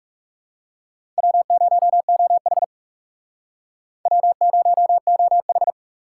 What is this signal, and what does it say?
Morse code, heard as a keyed tone, about 39 wpm, 700 Hz. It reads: W0OH W0OH